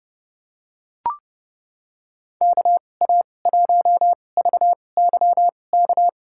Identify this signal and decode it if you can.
Morse code, heard as a keyed tone, about 30 wpm, spaced as for 22 wpm, 700 Hz. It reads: KA1VYK
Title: USA Call Signs - 2x Character Spacing - 30wpm